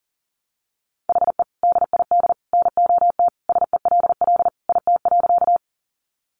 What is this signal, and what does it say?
Morse code, heard as a keyed tone, about 40 wpm, 700 Hz. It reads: HE DID NOT SELL IT.
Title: Sentences from Top 1100 Words 40wpm